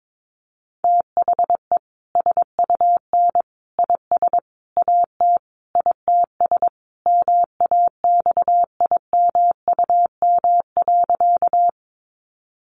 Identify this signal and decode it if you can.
Morse code, heard as a keyed tone, about 22 wpm, 700 Hz. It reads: THE SUN IS AT ITS MAXIMUM.
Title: Sentences from Top 1300 Words 22wpm